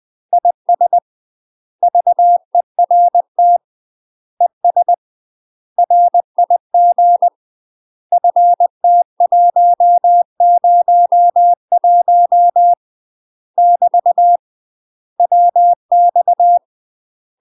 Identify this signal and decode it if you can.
Morse code, heard as a keyed tone, about 20 wpm, 700 Hz. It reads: IS VERT ES RIG FT101 <BT> WX
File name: UK MASTER QSOs TRAINING / 20WPM / UK MASTER QSOs 0002